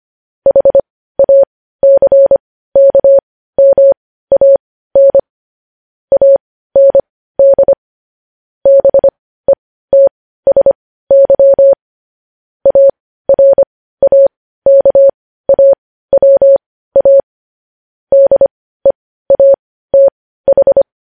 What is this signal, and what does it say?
Morse code, heard as a keyed tone, about 25 wpm, spaced as for 15 wpm, 550 Hz. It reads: HACKMAN AND BETSY ARAKAWA DEATH
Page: news headlines in morse code at 15 wpm